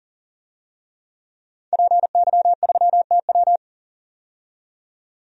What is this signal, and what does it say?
Morse code, heard as a keyed tone, about 40 wpm, 700 Hz. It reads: PY3TW
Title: DXCC-100 Call Signs - 5 Characters - 40wpm